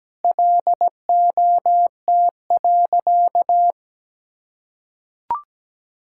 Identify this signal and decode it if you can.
Morse code, heard as a keyed tone, about 17 wpm, 700 Hz. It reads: LOT.
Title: Sentences from Top 1000 Words - Review Part 5 - 17wpm